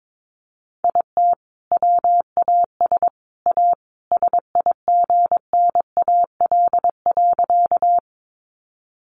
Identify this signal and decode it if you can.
Morse code, heard as a keyed tone, about 22 wpm, 700 Hz. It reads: IT WAS A SIGNAL.
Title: Sentences from Top 1400 Words 22wpm